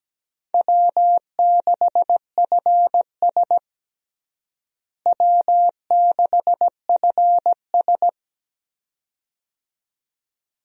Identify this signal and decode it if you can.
Morse code, heard as a keyed tone, about 17 wpm, 700 Hz. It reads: W6FS W6FS
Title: USA Call Signs - 4 Characters - Repeated 2x - 17wpm